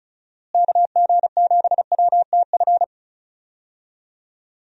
Morse code, heard as a keyed tone, about 34 wpm, 700 Hz. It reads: KG7WTF
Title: USA Call Signs - 6 Characters - Repeated 2x - 35wpm